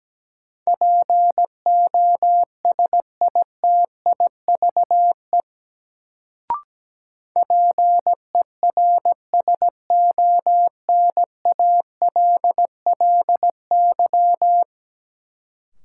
Morse code, heard as a keyed tone, about 17 wpm, 700 Hz. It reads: POSITIVE PERSONALLY
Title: Sentences from Top 1400 Words - New Word Drill Alt - 17wpm